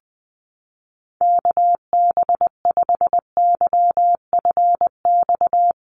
Morse code, heard as a keyed tone, about 20 wpm, 700 Hz. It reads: KB5YFX
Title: USA Call Signs Repeated 3x 20wpm